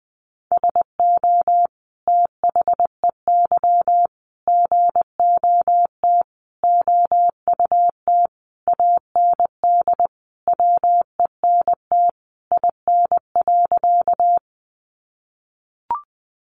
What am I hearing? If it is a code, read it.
Morse code, heard as a keyed tone, about 20 wpm, 700 Hz. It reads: SO THEY GOT OUT AND WENT IN.
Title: Sentences from Top 300 Words 20wpm